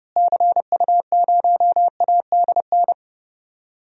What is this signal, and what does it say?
Morse code, heard as a keyed tone, about 30 wpm, 700 Hz. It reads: CU0ADN
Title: DXCC-100 Call Signs - 6 Characters - Repeated 3x - 30wpm